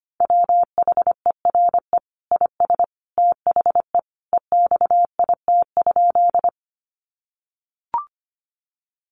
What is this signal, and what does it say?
Morse code, heard as a keyed tone, about 25 wpm, 700 Hz. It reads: WHERE IS THE EXIT?
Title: Sentences from Top 1100 Words 25wpm